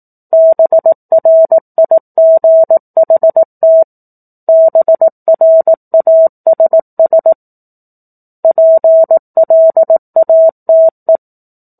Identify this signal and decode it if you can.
Morse code, heard as a keyed tone, about 18 wpm, 650 Hz. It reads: BRIGHT BRASS PLATE
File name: 04-Chapter-04 18 WPM